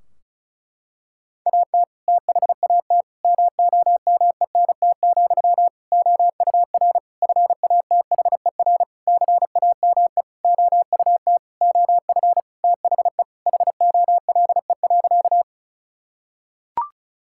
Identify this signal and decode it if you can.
Morse code, heard as a keyed tone, about 34 wpm, 700 Hz. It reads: AT THAT MOMENT, OUR FATHER CAME OUT OF THE HOLE.
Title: Sentences from Top 1600 Words 35wpm